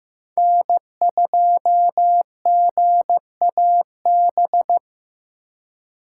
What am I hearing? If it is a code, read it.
Morse code, heard as a keyed tone, about 18 wpm, 700 Hz. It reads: N2GAB
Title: Speed Racing - USA Call Signs 15wpm